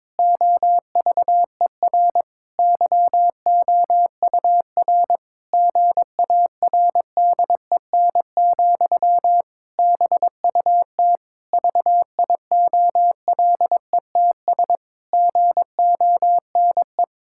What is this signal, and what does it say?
Morse code, heard as a keyed tone, about 22 wpm, 700 Hz. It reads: OVER YOUR GARDEN, BUT VIOLETS GONE
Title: Pangram Challenge v2 - 22wpm